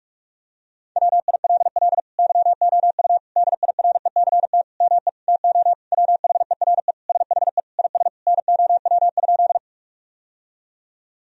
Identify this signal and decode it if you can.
Morse code, heard as a keyed tone, about 42 wpm, 700 Hz. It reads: WILL YOU DIRECT ME TO WHERE SHE IS NOW?
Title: Sentences from Top 1000 Words - Review Part 2 - 45wpm